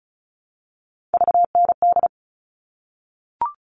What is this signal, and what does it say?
Morse code, heard as a keyed tone, about 34 wpm, 700 Hz. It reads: VND
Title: All Cut Numbers - 3-Digit 35wpm